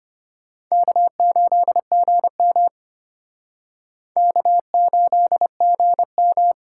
Morse code, heard as a keyed tone, about 31 wpm, 700 Hz. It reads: K8GM K8GM
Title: Speed Racing - USA 1x2 Call Signs 25wpm